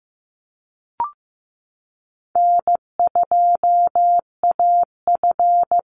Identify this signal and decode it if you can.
Morse code, heard as a keyed tone, about 15 wpm, 700 Hz. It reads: N2AF
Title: NA SOTA Chaser Callsigns 15wpm